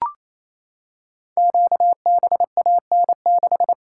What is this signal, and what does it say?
Morse code, heard as a keyed tone, about 27 wpm, 700 Hz. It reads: QBAN6
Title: Code Groups - 5 Letter:Numbers - 28wpm